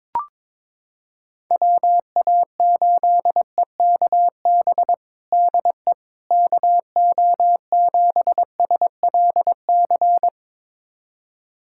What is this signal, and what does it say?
Morse code, heard as a keyed tone, about 22 wpm, 700 Hz. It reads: WA8EKB DE KO7SLC
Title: USA Callsign DE Callsign - 6 Characters - 22wpm